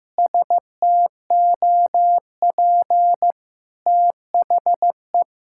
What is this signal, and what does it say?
Morse code, heard as a keyed tone, about 15 wpm, 700 Hz. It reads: STOP THE
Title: Sentences from Top 1000 Words - Review Part 6 - 15wpm